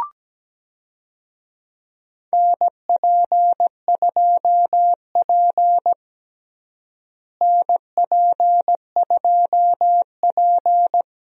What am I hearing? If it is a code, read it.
Morse code, heard as a keyed tone, about 17 wpm, 700 Hz. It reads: NP2P NP2P
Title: USA Call Signs - 4 Characters - Repeated 3x - 17wpm